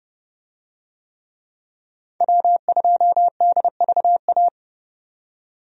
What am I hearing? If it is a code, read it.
Morse code, heard as a keyed tone, about 30 wpm, 700 Hz. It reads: W2DVA
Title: USA Call Signs - 5 Characters - 30wpm